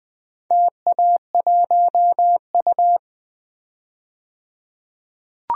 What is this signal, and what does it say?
Morse code, heard as a keyed tone, about 20 wpm, 700 Hz. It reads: TA1U
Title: DXCC-100 Call Signs - 4 Characters - 20wpm